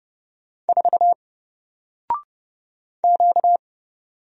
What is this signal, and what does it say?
Morse code, heard as a keyed tone, about 31 wpm, 700 Hz. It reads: 4 Q
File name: Single-Letter-Number-30wpm